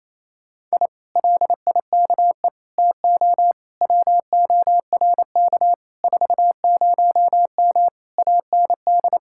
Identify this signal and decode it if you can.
Morse code, heard as a keyed tone, about 31 wpm, 700 Hz. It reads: I LIKE TO WORK 40M AND
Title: Speed-Racing Common QSO Phrases - 28wpm